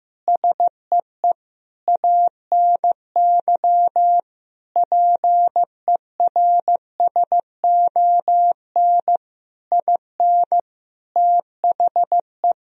Morse code, heard as a keyed tone, about 16 wpm, 700 Hz. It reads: SEE ANY PERSON IN THE
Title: Sentences from Top 1000 Words - Review Part 3 - 15wpm